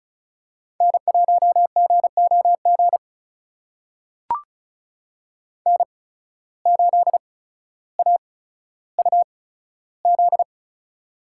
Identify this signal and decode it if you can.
Morse code, heard as a keyed tone, about 36 wpm, spaced as for 15 wpm, 700 Hz. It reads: N1GOG N8AUZ
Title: USA Call Signs - 5 Characters - 8x Character Spacing - 35wpm